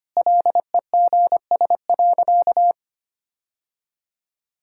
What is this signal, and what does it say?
Morse code, heard as a keyed tone, about 25 wpm, 690 Hz. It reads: LEGS.
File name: Easy Sentences with Hints - 25wpm